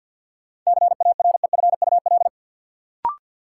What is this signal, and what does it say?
Morse code, heard as a keyed tone, about 53 wpm, 700 Hz. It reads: CAREFUL
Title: Sentences from Top 1600 Words - New Word Drill Alt - 50wpm